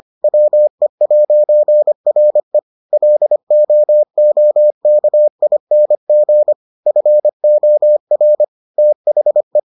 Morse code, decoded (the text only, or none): WE'RE LOOKING FOR THE